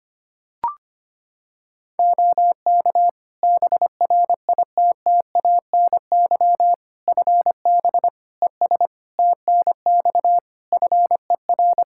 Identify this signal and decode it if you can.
Morse code, heard as a keyed tone, about 25 wpm, 700 Hz. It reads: OK BRITTANY FB ES TNX FER
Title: K1USN SST QSO Practice 25wpm